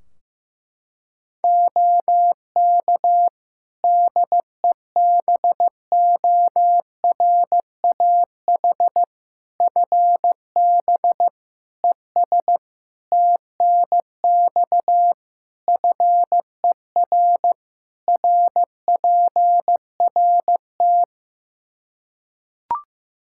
Morse code, heard as a keyed tone, about 15 wpm, 700 Hz. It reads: OK DEBORAH FB ES TNX FER RPRT
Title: Practice QSO - Exchange 2 - Info - 15wpm